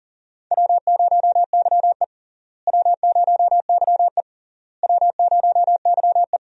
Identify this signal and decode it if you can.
Morse code, heard as a keyed tone, about 40 wpm, 700 Hz. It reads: W0YE W0YE W0YE
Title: USA Call Signs - 4 Characters - Repeated 3x - 40wpm